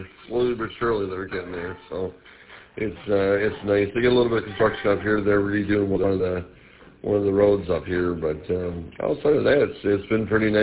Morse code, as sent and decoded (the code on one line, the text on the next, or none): none